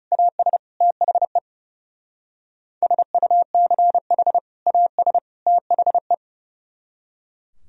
Morse code, heard as a keyed tone, about 34 wpm, 700 Hz. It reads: AS THE SUCH AS THE
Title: Speed-Racing - Sets of 3 Words 30wpm